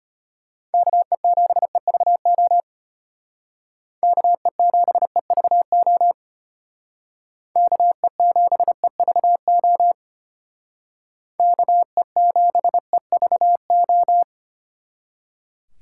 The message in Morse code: KE7EVO KE7EVO KE7EVO KE7EVO

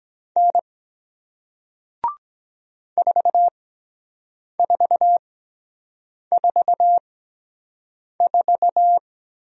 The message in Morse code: N 4444

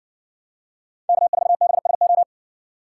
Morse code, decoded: K4BSO

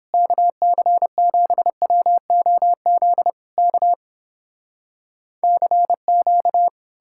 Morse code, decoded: KC7WOZ K CQ

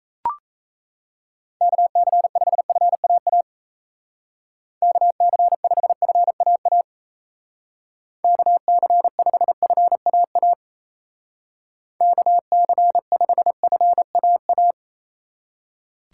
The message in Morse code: KC5FAA KC5FAA KC5FAA KC5FAA

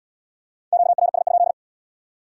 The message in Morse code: QRS?

